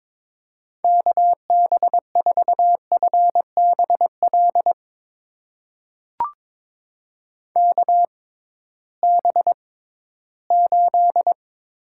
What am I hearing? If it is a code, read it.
Morse code, heard as a keyed tone, about 22 wpm, spaced as for 14 wpm, 700 Hz. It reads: KB4FBL KB8
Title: USA Call Signs - 6 Characters - 6x Character Spacing - 22wpm